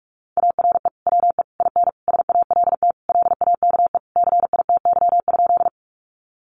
Morse code, heard as a keyed tone, about 44 wpm, 700 Hz. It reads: ARE WE IN SALT LAKE CITY?
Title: Sentences from Top 1100 Words 45wpm